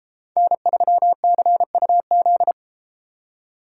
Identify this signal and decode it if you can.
Morse code, heard as a keyed tone, about 33 wpm, 700 Hz. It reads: N3CUZ